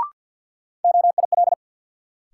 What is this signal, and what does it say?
Morse code, heard as a keyed tone, about 51 wpm, 700 Hz. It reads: OIL
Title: Sentences from Top 1000 Words - Review Part 4 - New Word Drill - 50wpm